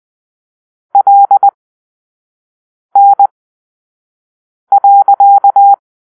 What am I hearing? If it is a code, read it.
Morse code, heard as a keyed tone, about 20 wpm, spaced as for 5 wpm, 800 Hz. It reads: LN.
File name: arrl / Grp-6 Drill-4